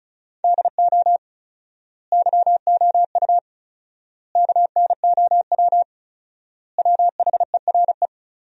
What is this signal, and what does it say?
Morse code, heard as a keyed tone, about 36 wpm, 700 Hz. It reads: DO YOU KNOW WHERE
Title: 4x Word-Spacing - Sentences from Top 300 Words - 35wpm